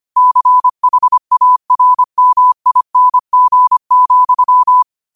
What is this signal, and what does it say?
Morse code, heard as a keyed tone, about 25 wpm, 1000 Hz. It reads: CHARMING,
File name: 25wpm